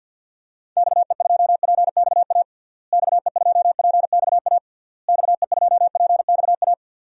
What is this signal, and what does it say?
Morse code, heard as a keyed tone, about 52 wpm, 700 Hz. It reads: XE2PXA XE2PXA XE2PXA
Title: DXCC-100 Call Signs - 6 Characters - Repeated 3x - 50wpm